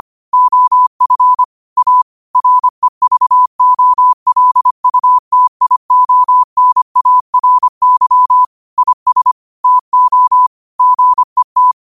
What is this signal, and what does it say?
Morse code, heard as a keyed tone, about 25 wpm, 1000 Hz. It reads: OF A REVOLUTIONARY IS TO GET
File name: qotdfiles / 25wpm